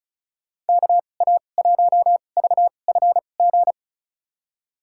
Morse code, heard as a keyed tone, about 34 wpm, spaced as for 25 wpm, 700 Hz. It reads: KA1VFG
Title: USA Call Signs - 6 Characters - 2x Character Spacing - 35wpm